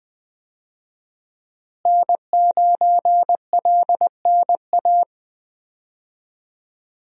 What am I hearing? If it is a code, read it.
Morse code, heard as a keyed tone, about 20 wpm, 700 Hz. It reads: N9LNA
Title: USA Call Signs - 5 Characters - Repeated 2x - 20wpm